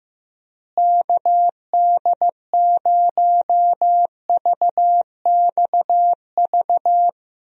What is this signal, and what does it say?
Morse code, heard as a keyed tone, about 15 wpm, 700 Hz. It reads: KD0VXV
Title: USA Call Signs - 6 Characters - Repeated 3x - 15wpm